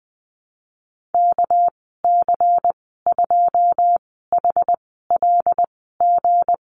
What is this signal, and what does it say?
Morse code, heard as a keyed tone, about 19 wpm, spaced as for 14 wpm, 700 Hz. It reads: KC2HLG
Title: 231 Lesson 37g - Letter x - Call Signs 20wpm